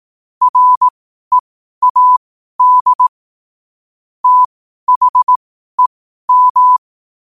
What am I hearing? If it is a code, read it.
Morse code, heard as a keyed tone, about 18 wpm, spaced as for 12 wpm, 1000 Hz. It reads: READ THEM
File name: qotdfiles / 12wpm